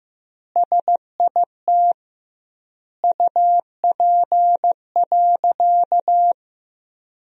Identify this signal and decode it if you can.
Morse code, heard as a keyed tone, about 15 wpm, 700 Hz. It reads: SIT UP.
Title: 2x Word-Spacing - Sentences from Top 500 Words - 15wpm